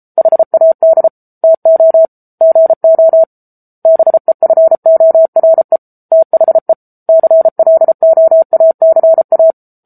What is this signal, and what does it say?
Morse code, heard as a keyed tone, about 34 wpm, 650 Hz. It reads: HAD TO GO BEFORE THE CLOACA
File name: Book Two / 07-Chapter 7 - 33 WPM